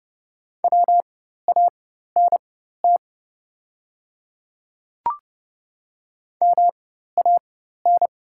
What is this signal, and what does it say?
Morse code, heard as a keyed tone, about 30 wpm, spaced as for 14 wpm, 700 Hz. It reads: WANT MAN
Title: 4x Character Spacing - Top 200 Words - 30wpm